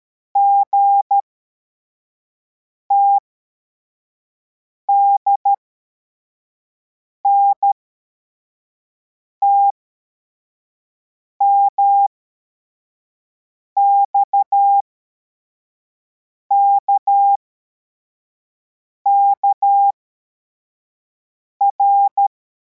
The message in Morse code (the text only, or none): GTDNT MXKKR